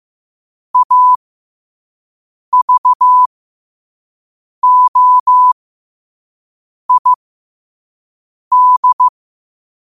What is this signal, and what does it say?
Morse code, heard as a keyed tone, about 15 wpm, spaced as for 5 wpm, 1000 Hz. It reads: AVOID
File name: qotdfiles / 5wpm